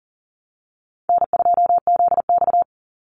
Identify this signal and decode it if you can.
Morse code, heard as a keyed tone, about 40 wpm, 700 Hz. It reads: N2ZX
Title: Easy Call Signs Repeated 2x - 40wpm